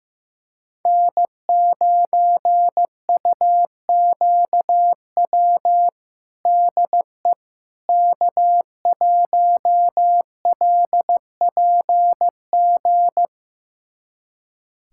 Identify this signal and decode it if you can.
Morse code, heard as a keyed tone, about 15 wpm, 700 Hz. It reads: N9UQW DE K1LPG
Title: USA Callsign DE Callsign - 5 Characters - 15wpm